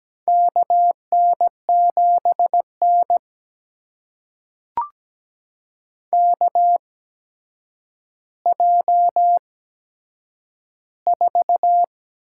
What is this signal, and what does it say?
Morse code, heard as a keyed tone, about 17 wpm, spaced as for 8 wpm, 700 Hz. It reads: KN7N KJ4
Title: USA Call Signs - 8x Character Spacing - 17wpm